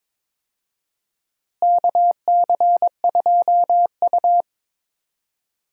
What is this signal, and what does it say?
Morse code, heard as a keyed tone, about 22 wpm, 700 Hz. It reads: KC2U
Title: USA Call Signs - 4 Characters - Repeated 2x - 22wpm